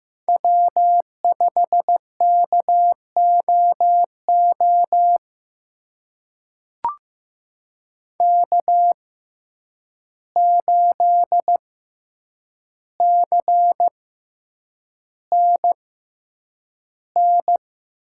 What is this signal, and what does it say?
Morse code, heard as a keyed tone, about 15 wpm, spaced as for 8 wpm, 700 Hz. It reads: W5KOO K8CNN
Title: USA Call Signs - 6x Character Spacing - 15wpm